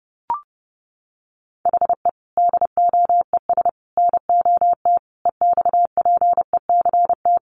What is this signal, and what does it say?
Morse code, heard as a keyed tone, about 30 wpm, 700 Hz. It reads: HE DOES NOT EXPECT
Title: Sentences from Top 1100 Words 30wpm